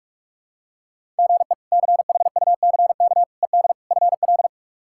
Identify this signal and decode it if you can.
Morse code, heard as a keyed tone, about 42 wpm, 700 Hz. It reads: GE CHUCK ED FL
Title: K1USN SST QSO Practice - 45wpm